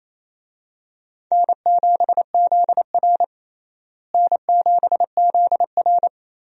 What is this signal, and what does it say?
Morse code, heard as a keyed tone, about 28 wpm, 700 Hz. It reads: N7ZR N7ZR
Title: USA Call Signs - 4 Characters - Repeated 2x - 28wpm